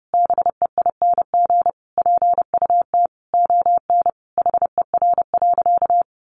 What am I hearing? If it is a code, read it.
Morse code, heard as a keyed tone, about 30 wpm, 700 Hz. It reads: BEING PUT ON HER.
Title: Sentences from Top 1300 Words 30wpm